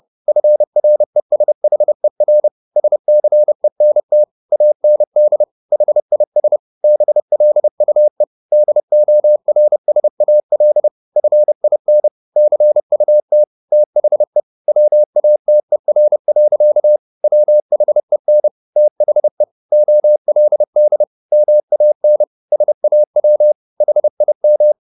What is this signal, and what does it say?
Morse code, heard as a keyed tone, about 30 wpm, 600 Hz. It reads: FRESHER SCENT AND HIS BLUE DORSAL FIN CUT THE WATER. WHEN THE OLD MAN SAW HIM